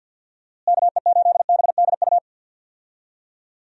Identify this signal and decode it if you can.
Morse code, heard as a keyed tone, about 50 wpm, 700 Hz. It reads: KE8BDU
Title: USA Call Signs - 6 Characters - Repeated 3x - 50wpm